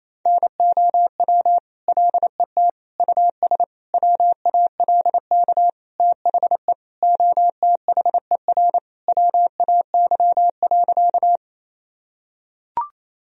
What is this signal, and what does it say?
Morse code, heard as a keyed tone, about 28 wpm, 700 Hz. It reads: NOW LET US WALK THE OTHER WAY.
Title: Sentences from Top 1000 Words - Review Part 1 - 28wpm